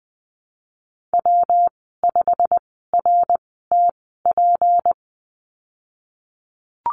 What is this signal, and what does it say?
Morse code, heard as a keyed tone, about 20 wpm, spaced as for 15 wpm, 700 Hz. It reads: W5RTP
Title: 167 Lesson 27g - Number 7 - Call Signs 20wpm